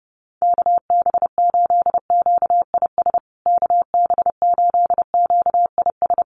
Morse code, heard as a keyed tone, about 30 wpm, 700 Hz. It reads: KB8QIS KB8QIS
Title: USA Call Signs Repeated 3x 30wpm